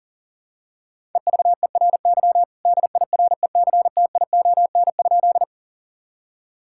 Morse code, heard as a keyed tone, about 40 wpm, 700 Hz. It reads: EVERY DIRECTION?
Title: Sentences from Top 1000 Words - Review Part 5 - 40wpm